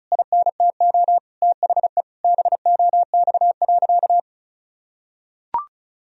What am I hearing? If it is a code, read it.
Morse code, heard as a keyed tone, about 34 wpm, 700 Hz. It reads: INTO THE BOX.
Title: Sentences from Top 1000 Words - Review Part 2 - 35wpm